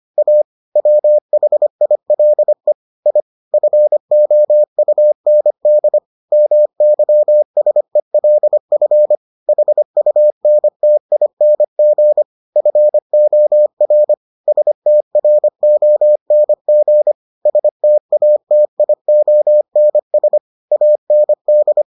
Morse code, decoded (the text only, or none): A WHILE I FOUND MYSELF HUNTING FOR STRONG STATIONS AND